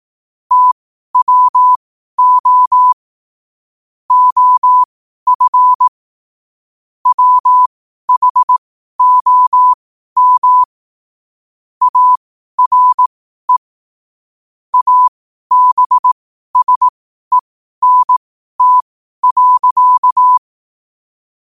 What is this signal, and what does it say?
Morse code, heard as a keyed tone, about 18 wpm, spaced as for 12 wpm, 1000 Hz. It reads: TWO OF WHOM ARE ABSENT.
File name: qotdfiles / 12wpm